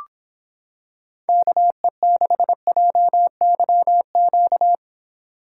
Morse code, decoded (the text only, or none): KE6JYQ